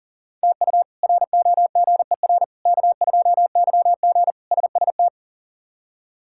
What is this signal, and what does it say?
Morse code, heard as a keyed tone, about 40 wpm, 700 Hz. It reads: TU ROGER K2YG SST
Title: K1USN SST QSO Practice - 40wpm